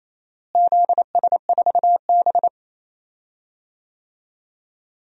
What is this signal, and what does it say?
Morse code, heard as a keyed tone, about 28 wpm, 700 Hz. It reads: ZS4B
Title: DXCC-100 Call Signs - 4 Characters - Repeated 2x - 28wpm